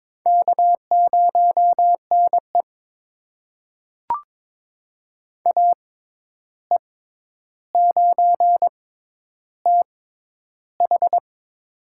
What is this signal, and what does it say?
Morse code, heard as a keyed tone, about 22 wpm, spaced as for 10 wpm, 700 Hz. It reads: K0NE AE9TH
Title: USA Call Signs - 6x Character Spacing - 22wpm